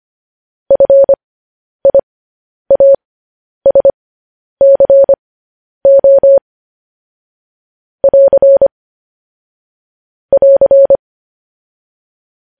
Morse code, heard as a keyed tone, about 25 wpm, spaced as for 10 wpm, 550 Hz. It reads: FIASCO <AR> <AR>
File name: podcast / News headlines-1x-25x10